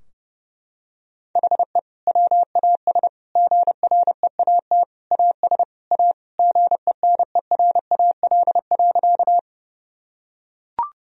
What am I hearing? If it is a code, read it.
Morse code, heard as a keyed tone, about 30 wpm, 700 Hz. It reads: HE WAS GREAT AS A GENERAL.
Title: Sentences from Top 1000 Words - Review Part 3 - 30wpm